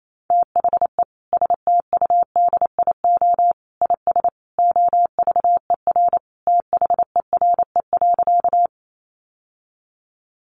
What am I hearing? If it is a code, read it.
Morse code, heard as a keyed tone, about 28 wpm, 700 Hz. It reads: THE STUDIO IS OVER THERE.
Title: Sentences from Top 1200 Words 28wpm